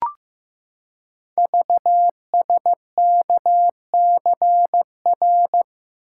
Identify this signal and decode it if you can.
Morse code, heard as a keyed tone, about 15 wpm, 700 Hz. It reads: VSKCR
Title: Code Groups - 5 Letters - 15wpm